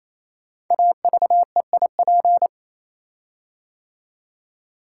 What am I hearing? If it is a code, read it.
Morse code, heard as a keyed tone, about 27 wpm, 700 Hz. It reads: AVEIP
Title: Code Groups - 5 Letters - 28wpm